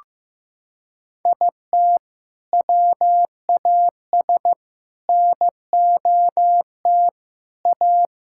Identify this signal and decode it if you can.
Morse code, heard as a keyed tone, about 15 wpm, 700 Hz. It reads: IT WAS NOT A
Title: Sentences from Top 1000 Words - Review Part 6 - 15wpm